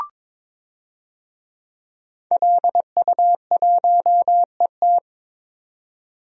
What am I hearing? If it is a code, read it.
Morse code, heard as a keyed tone, about 23 wpm, 700 Hz. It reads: LU1ET
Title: DXCC-100 Call Signs - 5 Characters - 22wpm